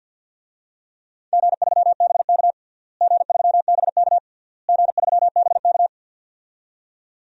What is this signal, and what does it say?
Morse code, heard as a keyed tone, about 47 wpm, 700 Hz. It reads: G3BK G3BK G3BK
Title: DXCC-100 Call Signs - 4 Characters - Repeated 3x - 50wpm